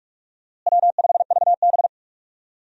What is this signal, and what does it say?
Morse code, heard as a keyed tone, about 47 wpm, 700 Hz. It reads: W5VB
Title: USA Call Signs - 4 Characters - Repeated 2x - 45wpm